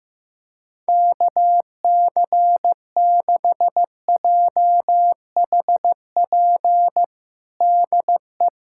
Morse code, heard as a keyed tone, about 16 wpm, 700 Hz. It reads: KC6JHP DE
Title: USA Callsign DE Callsign 15wpm